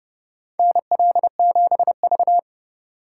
Morse code, heard as a keyed tone, about 30 wpm, 700 Hz. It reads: NL7V
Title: K1USN SST QSO Practice - 30wpm